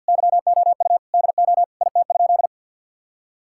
Morse code, heard as a keyed tone, about 49 wpm, 700 Hz. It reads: YOU DO IT?
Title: Sentences from Top 1000 Words - Review Part 5 - 50wpm